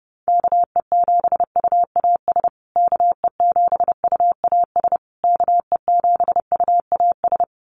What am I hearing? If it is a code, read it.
Morse code, heard as a keyed tone, about 30 wpm, 700 Hz. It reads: KE7UAS KE7UAS KE7UAS
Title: USA Call Signs Repeated 3x 30wpm